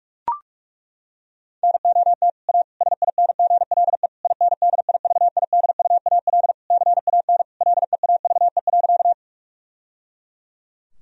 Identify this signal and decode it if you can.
Morse code, heard as a keyed tone, about 41 wpm, 700 Hz. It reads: NOT A SINGLE INDIVIDUAL CAN LEAVE.
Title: Sentences from Top 1000 Words - Review Part 4 - 45wpm